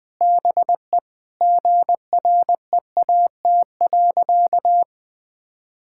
Morse code, heard as a keyed tone, about 20 wpm, 700 Hz. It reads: BE GREAT.
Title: Sentences with 2-Word Phrases - 20wpm